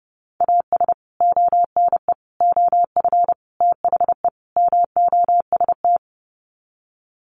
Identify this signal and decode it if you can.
Morse code, heard as a keyed tone, about 30 wpm, 700 Hz. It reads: AS ONE OF THE MOST
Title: Sets of 5 Words 30wpm